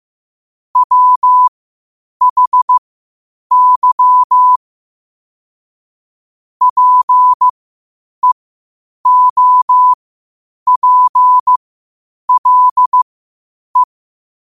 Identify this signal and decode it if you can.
Morse code, heard as a keyed tone, about 15 wpm, spaced as for 8 wpm, 1000 Hz. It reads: WHY PEOPLE
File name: qotdfiles / 8wpm